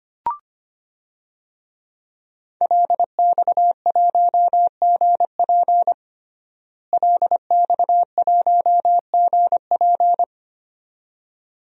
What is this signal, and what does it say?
Morse code, heard as a keyed tone, about 25 wpm, 700 Hz. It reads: LX1GP LX1GP
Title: DXCC-100 Call Signs - 5 Characters - Repeated 2x - 25wpm